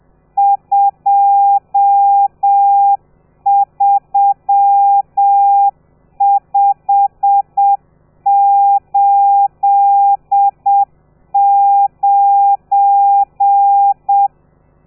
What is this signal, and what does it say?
Morse code, heard as a keyed tone, about 7 wpm, 800 Hz. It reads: 23589
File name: telegrafia / i2tfj / Lez12 30